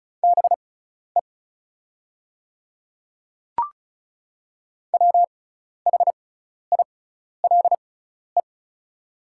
Morse code, heard as a keyed tone, about 36 wpm, spaced as for 12 wpm, 690 Hz. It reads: BE WHILE
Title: 6x Character Spacing - Top 300 Words - 35wpm